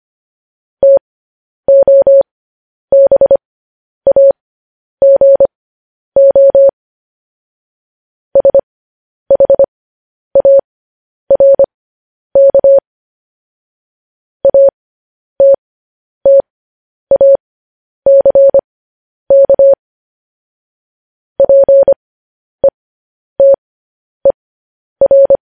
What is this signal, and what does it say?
Morse code, heard as a keyed tone, about 25 wpm, spaced as for 10 wpm, 550 Hz. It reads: TOBAGO SHARK ATTACK PETER